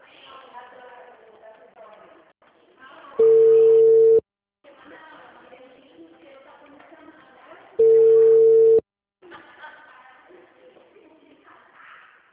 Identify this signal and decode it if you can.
Ringback tone (China, 450 Hz), with speech.